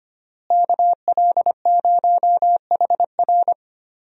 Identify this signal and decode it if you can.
Morse code, heard as a keyed tone, about 25 wpm, 700 Hz. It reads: KL0HR
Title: USA Call Signs - 5 Characters - Repeated 3x - 25wpm